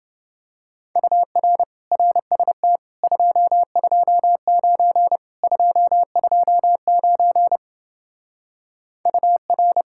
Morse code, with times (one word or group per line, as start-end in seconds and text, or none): 1.0-1.6 UR
1.9-2.8 RST
3.0-5.2 229
5.4-7.6 229
9.1-9.8 UR